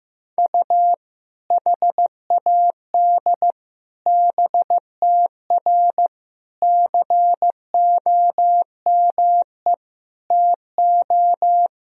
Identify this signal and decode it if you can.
Morse code, heard as a keyed tone, about 15 wpm, 700 Hz. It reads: U HAD BTR COME TO
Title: Sentences from Top 400 Words and Abbreviations 15wpm